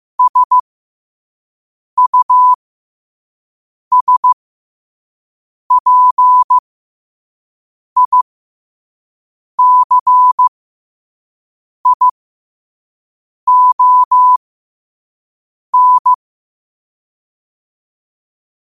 Morse code, heard as a keyed tone, about 15 wpm, spaced as for 5 wpm, 1000 Hz. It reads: SUSPICION
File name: qotdfiles / 5wpm